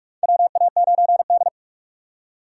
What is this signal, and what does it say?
Morse code, heard as a keyed tone, about 44 wpm, 690 Hz. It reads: WA9D
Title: USA Call Signs - 4 Characters - 2x Character Spacing - 45wpm